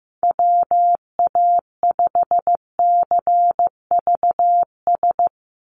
Morse code, heard as a keyed tone, about 15 wpm, 700 Hz. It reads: WA5CVS
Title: USA Call Signs 15wpm